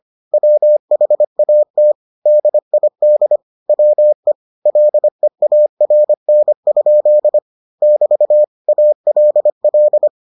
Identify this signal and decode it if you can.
Morse code, heard as a keyed tone, about 25 wpm, 600 Hz. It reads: WHAT DID WE LEARN? <BT> ALL